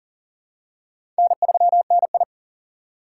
Morse code, heard as a keyed tone, about 38 wpm, 700 Hz. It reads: N3NI
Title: USA Call Signs - 4 Characters - 4x Character Spacing - 40wpm